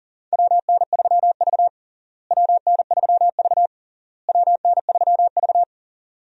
Morse code, heard as a keyed tone, about 40 wpm, 700 Hz. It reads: WN3V WN3V WN3V
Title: USA Call Signs - 4 Characters - Repeated 3x - 40wpm